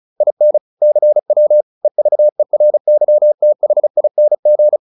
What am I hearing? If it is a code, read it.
Morse code, heard as a keyed tone, about 34 wpm, 600 Hz. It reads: IN CW EVERYTHING